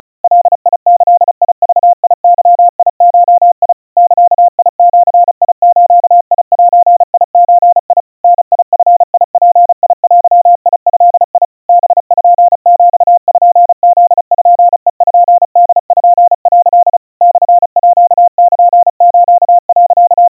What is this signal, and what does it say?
Morse code (Japanese Wabun code), heard as a keyed tone, about 34 wpm, 700 Hz. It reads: ガギグゲゴ ザジズゼゾ ダヂヅデド パピプペポ」 モールス、
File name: kawa / Morse sample / aiueo 120 cpm 700 Hz Crispy